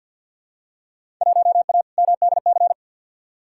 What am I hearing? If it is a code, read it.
Morse code, heard as a keyed tone, about 50 wpm, 700 Hz. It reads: 1A MDC
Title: Field Day Exchanges - 50wpm